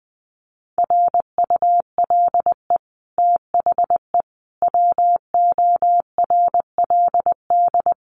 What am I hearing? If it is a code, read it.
Morse code, heard as a keyed tone, about 20 wpm, 700 Hz. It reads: RULE THE WORLD
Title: Sentences from Top 600 Words 20wpm